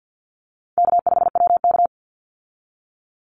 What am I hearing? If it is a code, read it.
Morse code, heard as a keyed tone, about 50 wpm, 700 Hz. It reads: K5WK
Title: Easy Call Signs Repeated 3x - 50wpm